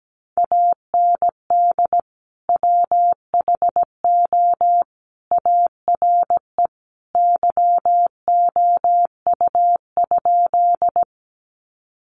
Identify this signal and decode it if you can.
Morse code, heard as a keyed tone, about 17 wpm, 700 Hz. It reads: AND WHO ARE YOU?
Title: Sentences from Top 100 Words 17wpm